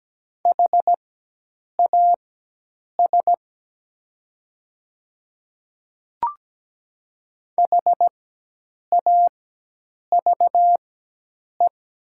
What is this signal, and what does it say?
Morse code, heard as a keyed tone, about 17 wpm, spaced as for 8 wpm, 700 Hz. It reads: HAS HAVE
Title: 4x Character Spacing - Top 200 Words - 17wpm